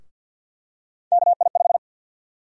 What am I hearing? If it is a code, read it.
Morse code, heard as a keyed tone, about 50 wpm, 700 Hz. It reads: KI5
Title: USA Call Sign Prefixes - 50wpm